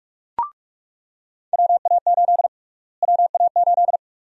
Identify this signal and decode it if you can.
Morse code, heard as a keyed tone, about 43 wpm, 700 Hz. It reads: WA8 WA8
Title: USA Call Sign Prefixes - Repeated 2x - 45wpm